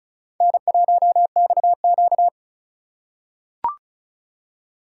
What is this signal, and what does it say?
Morse code, heard as a keyed tone, about 36 wpm, 700 Hz. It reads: N1XQ
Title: USA Call Signs - 4x Character Spacing - 35wpm